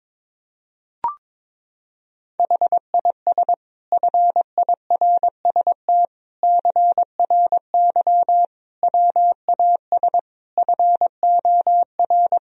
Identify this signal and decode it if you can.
Morse code, heard as a keyed tone, about 22 wpm, 700 Hz. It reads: HIS FIRST CRY WAS FOR
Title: Sentences from Top 1000 Words - Review Part 2 - 22wpm